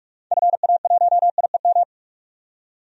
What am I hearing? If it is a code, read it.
Morse code, heard as a keyed tone, about 41 wpm, 700 Hz. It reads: FA1IEM
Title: DXCC-100 Call Signs - 6 Characters - 45wpm